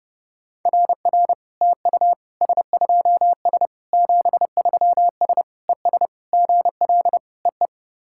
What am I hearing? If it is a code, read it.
Morse code, heard as a keyed tone, about 28 wpm, 660 Hz. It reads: RR TU S2S 73S ES GL EE
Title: SOTA QSO Practice 30wpm